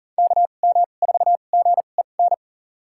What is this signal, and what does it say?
Morse code, heard as a keyed tone, about 40 wpm, spaced as for 29 wpm, 690 Hz. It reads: KM4GEN
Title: USA Call Signs - 6 Characters - 2x Character Spacing - 40wpm